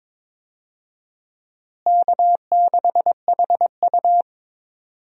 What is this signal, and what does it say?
Morse code, heard as a keyed tone, about 22 wpm, 700 Hz. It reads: K6HU
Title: USA Call Signs - 4 Characters - Repeated 3x - 22wpm